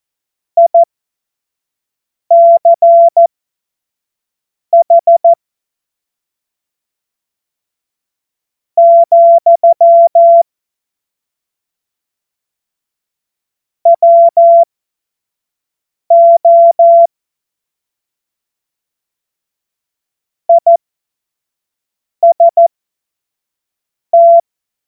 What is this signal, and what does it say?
Morse code, heard as a keyed tone, about 14 wpm, spaced as for 4 wpm, 680 Hz. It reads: ICH, WO IST
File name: Morsekurs / Witze / Witz 10. August 2021, 25 F70